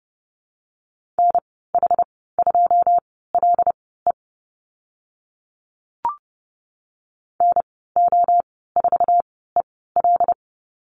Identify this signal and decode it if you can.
Morse code, heard as a keyed tone, about 30 wpm, spaced as for 17 wpm, 700 Hz. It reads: NH2LE NO4EL
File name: Lesson 14g - Number 2 - Call Signs 30wpm